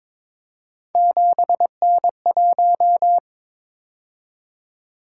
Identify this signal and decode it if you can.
Morse code, heard as a keyed tone, about 22 wpm, 700 Hz. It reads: 7N1
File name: practice / 22wpm / Three Letter-Number Combinations - 22wpm